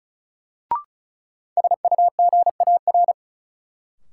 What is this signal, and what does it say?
Morse code, heard as a keyed tone, about 36 wpm, 700 Hz. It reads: SUGAR
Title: Sentences from Top 1200 Words - New Word Drill - 35wpm